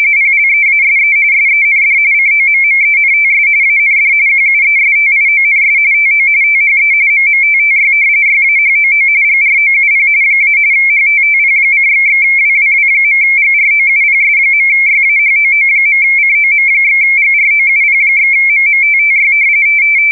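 RTTY, about 45.45 baud. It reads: PLEASE SEND COMMENTS AND QUESTIONS TO THE MMTTY YAHOO GROUP - HTTP://WWW.GROUPS.YAHOO.COM/GROUP/MMTTY
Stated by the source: RTTY test recording for the JE3HHT MMTTY engine used with UcxLog logging software.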